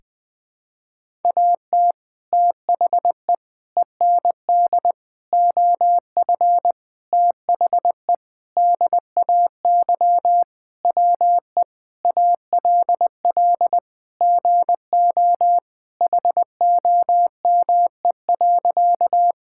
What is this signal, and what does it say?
Morse code, heard as a keyed tone, about 20 wpm, 700 Hz. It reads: AT THE END OF THE DAY WE ALL GO HOME.
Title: Sentences with 4-Word Phrases - Spoken Beginning - 20wpm